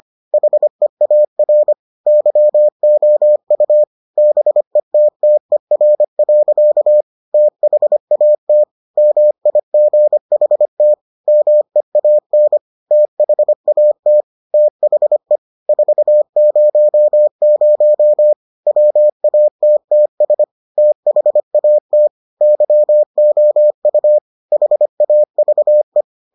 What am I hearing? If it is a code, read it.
Morse code, heard as a keyed tone, about 25 wpm, 600 Hz. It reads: HEAR YOU BETTER. THAT MIGHT MEAN THAT THE 400 WATTS THAT YOU HAVE